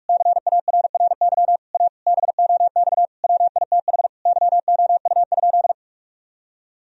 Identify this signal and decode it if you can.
Morse code, heard as a keyed tone, about 41 wpm, 700 Hz. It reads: CARRY A BOX WITH YOU?
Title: Sentences from Top 1000 Words - Review Part 2 - 45wpm